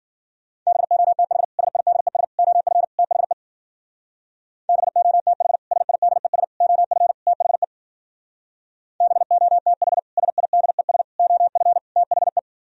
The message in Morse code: BOTH SIDES OF THE BOTH SIDES OF THE BOTH SIDES OF THE

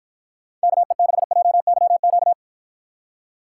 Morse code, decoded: KE6JYQ

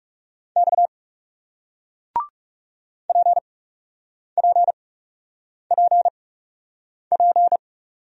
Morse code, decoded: X P P P P

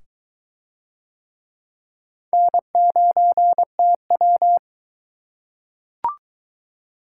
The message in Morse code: N9TW

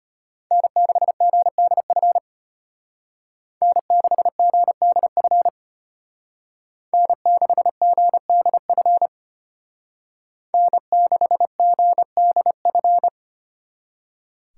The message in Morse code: N6GDF N6GDF N6GDF N6GDF

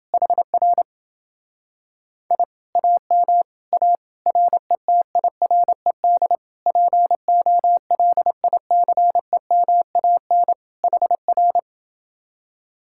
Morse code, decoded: HR I AM A RETIRED POLICEMAN HR

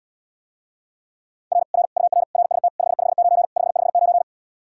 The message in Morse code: R R UR RST 559 559

